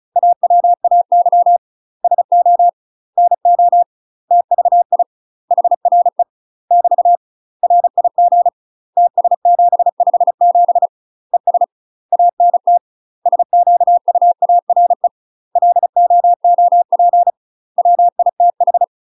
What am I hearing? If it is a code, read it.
Morse code, heard as a keyed tone, about 34 wpm, 700 Hz. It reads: AWAY SO NO TVI HRE <BT> RIG TS757 ES ANT SQUARE LOOP WITH
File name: UK MASTER QSOs TRAINING / 35WPM / UK MASTER QSOs 0000